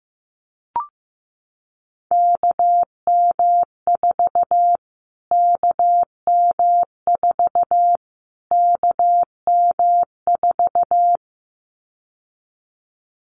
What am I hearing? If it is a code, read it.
Morse code, heard as a keyed tone, about 15 wpm, 700 Hz. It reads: KM4 KM4 KM4
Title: Callsign Prefixes for Top 25 DXCC - Repeated 3x - 15wpm